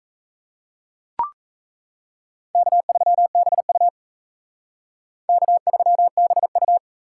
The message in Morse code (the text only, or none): K3BU K3BU